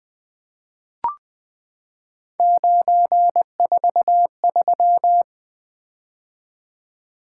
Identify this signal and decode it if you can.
Morse code, heard as a keyed tone, about 20 wpm, 700 Hz. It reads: 943
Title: Three Letter-Number Combinations - 20wpm